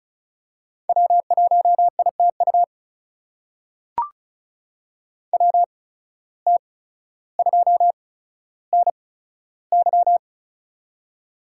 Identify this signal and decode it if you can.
Morse code, heard as a keyed tone, about 36 wpm, spaced as for 15 wpm, 700 Hz. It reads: W1ITU WT2NY
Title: USA Call Signs - 5 Characters - 8x Character Spacing - 35wpm